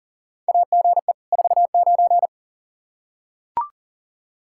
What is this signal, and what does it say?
Morse code, heard as a keyed tone, about 40 wpm, 700 Hz. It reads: AGE 49
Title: Practice QSO - Exchange 3 - More Info - 40wpm